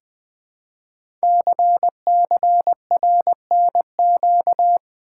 Morse code, heard as a keyed tone, about 20 wpm, 700 Hz. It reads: CCRNQ
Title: Code Groups - 5 Letter:Numbers - 20wpm